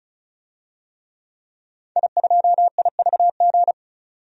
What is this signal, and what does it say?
Morse code, heard as a keyed tone, about 36 wpm, 700 Hz. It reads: I2IVG
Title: DXCC-100 Call Signs - 5 Characters - 35wpm